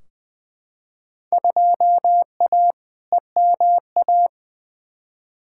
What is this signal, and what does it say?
Morse code, heard as a keyed tone, about 20 wpm, 700 Hz. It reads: 2A EMA
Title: Field Day Exchanges - 20wpm